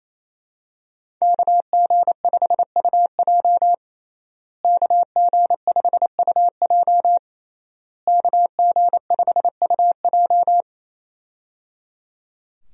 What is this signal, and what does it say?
Morse code, heard as a keyed tone, about 28 wpm, 700 Hz. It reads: KG5UJ KG5UJ KG5UJ
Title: USA Call Signs - 5 Characters - Repeated 3x - 28wpm